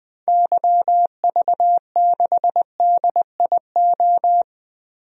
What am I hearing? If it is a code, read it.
Morse code, heard as a keyed tone, about 21 wpm, 700 Hz. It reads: YV6DIO
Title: DXCC-100 Call Signs - 6 Characters - 20wpm